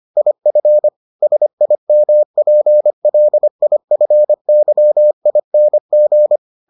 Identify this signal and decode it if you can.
Morse code, heard as a keyed tone, about 25 wpm, 600 Hz. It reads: IF SIMPLIFYING